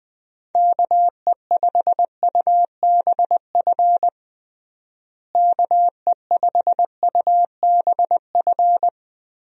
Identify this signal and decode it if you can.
Morse code, heard as a keyed tone, about 19 wpm, 700 Hz. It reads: KE5UBF KE5UBF
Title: USA Call Signs - 6 Characters - Repeated 2x - 20wpm